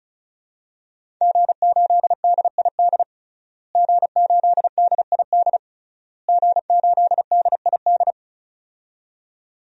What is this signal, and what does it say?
Morse code, heard as a keyed tone, about 36 wpm, 700 Hz. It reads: G8DID G8DID G8DID
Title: DXCC-100 Call Signs - 5 Characters - Repeated 3x - 35wpm